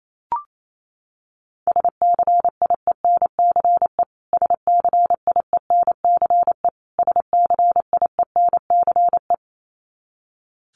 Morse code, heard as a keyed tone, about 28 wpm, 700 Hz. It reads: SCIENCE SCIENCE SCIENCE
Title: Top 500 Words Repeated 28wpm